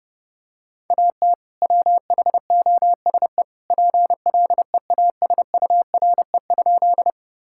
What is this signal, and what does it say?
Morse code, heard as a keyed tone, about 30 wpm, 700 Hz. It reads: AT WHOSE PLEASURE?
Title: Sentences from Top 1600 Words 30wpm